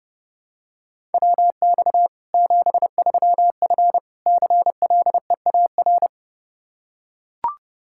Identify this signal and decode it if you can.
Morse code, heard as a keyed tone, about 30 wpm, 700 Hz. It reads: WX 73F CLEAR
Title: Practice QSO - Exchange 2 - Info - 30wpm